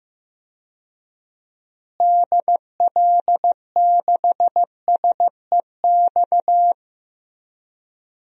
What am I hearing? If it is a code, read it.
Morse code, heard as a keyed tone, about 16 wpm, 700 Hz. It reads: DL6SEX
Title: DXCC-100 Call Signs - 6 Characters - Repeated 2x - 15wpm